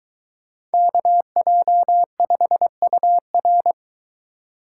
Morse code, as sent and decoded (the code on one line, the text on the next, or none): -.- .--- ..... ..- .-.
KJ5UR